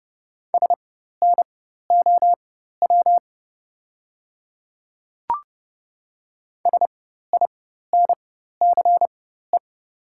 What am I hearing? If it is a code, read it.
Morse code, heard as a keyed tone, about 30 wpm, spaced as for 14 wpm, 690 Hz. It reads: SNOW SINCE
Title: 4x Character Spacing - Top 500 Words - 30wpm